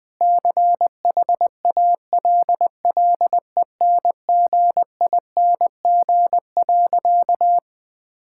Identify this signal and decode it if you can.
Morse code, heard as a keyed tone, about 20 wpm, 700 Hz. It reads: CHALLENGING.
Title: Sentences with 5-Word Phrases - Spoken Beginning - 20wpm